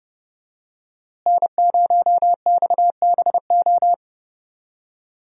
Morse code, heard as a keyed tone, about 30 wpm, 700 Hz. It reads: N0XBO
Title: USA Call Signs - 5 Characters - Repeated 2x - 30wpm